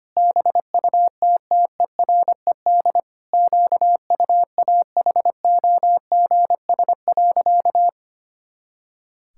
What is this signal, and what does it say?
Morse code, heard as a keyed tone, about 25 wpm, 700 Hz. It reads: BUTTERED QUAHOGS.
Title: Pangram Challenge v2 - 25wpm